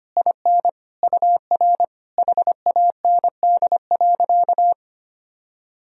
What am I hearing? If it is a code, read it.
Morse code, heard as a keyed tone, about 25 wpm, 700 Hz. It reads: IN UR HAND.
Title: Sentences from Top 400 Words and Abbreviations 25wpm